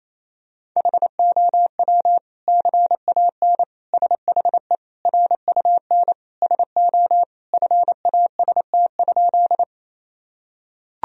Morse code, heard as a keyed tone, about 28 wpm, 700 Hz. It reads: HOW CAN SHE RUN SO FAST?
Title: Sentences from Top 1000 Words - Review Part 2 - 28wpm